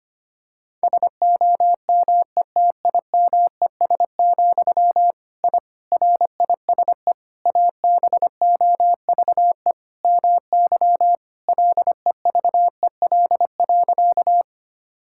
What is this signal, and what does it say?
Morse code, heard as a keyed tone, about 25 wpm, 700 Hz. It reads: SOMETIMES, I RISE ABOVE MY LEVEL.
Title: Sentences from Top 1000 Words - Review Part 6 - 25wpm